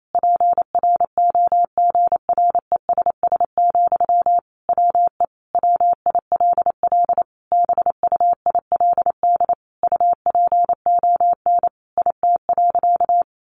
Morse code, heard as a keyed tone, about 28 wpm, 700 Hz. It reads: PROGRESS, WE WILL BUILD UPON IT.
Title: Sentences from Top 1400 Words 28wpm